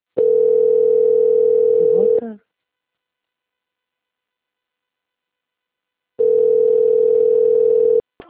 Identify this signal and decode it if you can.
Ringback tone (North America), 440 + 480 Hz, with speech.
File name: out- - - - - .12181